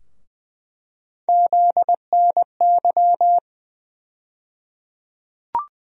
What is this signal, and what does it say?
Morse code, heard as a keyed tone, about 20 wpm, 700 Hz. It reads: ZNY
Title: USA Call Sign Postfixes - 3 Letters 20wpm